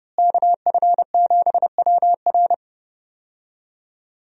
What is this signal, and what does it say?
Morse code, heard as a keyed tone, about 30 wpm, 700 Hz. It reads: KF7WR
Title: USA Call Signs - 5 Characters - Repeated 3x - 30wpm